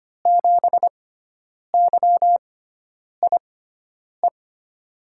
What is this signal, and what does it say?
Morse code, heard as a keyed tone, about 25 wpm, spaced as for 9 wpm, 700 Hz. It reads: 7YIE
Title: USA Call Signs - 6 Characters - 6x Character Spacing - 25wpm